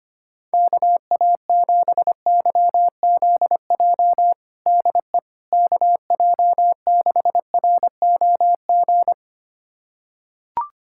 Morse code, heard as a keyed tone, about 25 wpm, 700 Hz. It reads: KA7YZJ DE KJ6ROG
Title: USA Callsign DE Callsign - 6 Characters - 25wpm